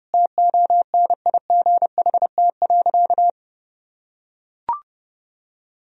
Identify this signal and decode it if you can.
Morse code, heard as a keyed tone, about 32 wpm, 700 Hz. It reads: TONIGHT.
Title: Sentences from Top 1000 Words - Review Part 4 - 30wpm